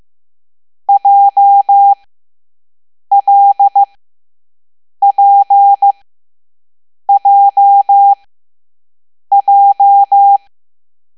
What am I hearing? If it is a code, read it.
Morse code, heard as a keyed tone, about 15 wpm, spaced as for 6 wpm, 800 Hz. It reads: JLPJJ